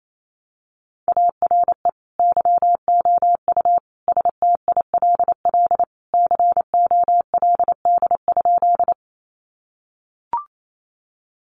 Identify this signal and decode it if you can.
Morse code, heard as a keyed tone, about 28 wpm, 700 Hz. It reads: ARE YOU STILL COLD?
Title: Sentences from Top 400 Words 28wpm